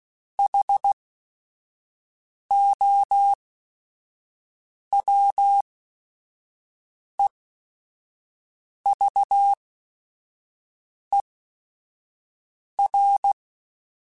Morse code, heard as a keyed tone, about 16 wpm, spaced as for 5 wpm, 800 Hz. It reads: HOWEVER